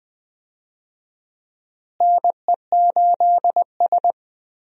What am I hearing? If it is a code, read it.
Morse code, heard as a keyed tone, about 21 wpm, 700 Hz. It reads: NE8S
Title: USA Call Signs - 4 Characters - Repeated 3x - 20wpm